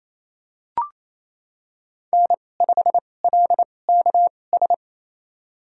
Morse code, heard as a keyed tone, about 28 wpm, spaced as for 20 wpm, 700 Hz. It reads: N5LKS
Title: USA Call Signs - 5 Characters - 2x Character Spacing - 28wpm